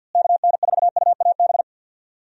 Morse code, heard as a keyed tone, about 52 wpm, 700 Hz. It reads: KN4UAB
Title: USA Call Signs - 6 Characters - Repeated 2x - 50wpm